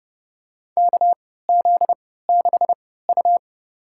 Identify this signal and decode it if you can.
Morse code, heard as a keyed tone, about 32 wpm, spaced as for 17 wpm, 700 Hz. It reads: KZ6U
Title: USA Call Signs - 3x Character Spacing - 30wpm